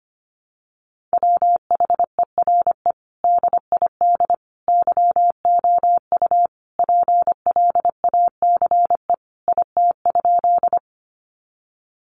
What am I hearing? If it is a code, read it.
Morse code, heard as a keyed tone, about 25 wpm, 700 Hz. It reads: WHERE DID YOU PLACE IT?
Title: Sentences from Top 200 Words 25wpm